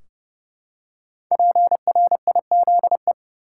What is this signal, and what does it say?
Morse code, heard as a keyed tone, about 32 wpm, 700 Hz. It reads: PRIZE
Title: Five Letter Words - 30wpm